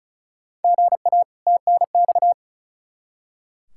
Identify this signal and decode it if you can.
Morse code, heard as a keyed tone, about 36 wpm, 700 Hz. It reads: GA TNX
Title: Practice QSO - Exchange 1 - Beginning - 35wpm